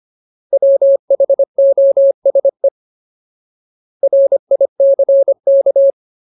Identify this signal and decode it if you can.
Morse code, heard as a keyed tone, about 25 wpm, 550 Hz. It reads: WHOSE RICK
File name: Ultimate Word List / 25WPM Ultimate